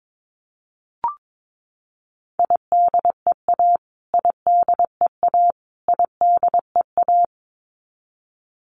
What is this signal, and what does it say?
Morse code, heard as a keyed tone, about 22 wpm, 700 Hz. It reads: IDEA IDEA IDEA
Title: Top 500 Words Repeated 22wpm